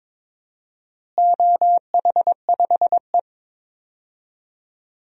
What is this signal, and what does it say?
Morse code, heard as a keyed tone, about 22 wpm, 700 Hz. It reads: OH5E
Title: DXCC-100 Call Signs - 4 Characters - Repeated 2x - 22wpm